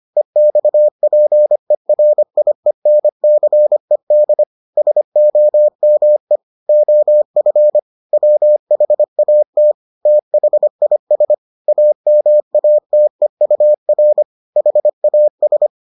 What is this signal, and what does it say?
Morse code, heard as a keyed tone, about 25 wpm, 600 Hz. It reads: EXPERIENCED SOME OF WHAT THIS AMATEUR HAS